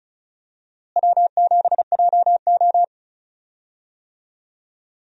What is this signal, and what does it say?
Morse code, heard as a keyed tone, about 34 wpm, 700 Hz. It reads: W7JO
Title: USA Call Signs - 4 Characters - Repeated 2x - 35wpm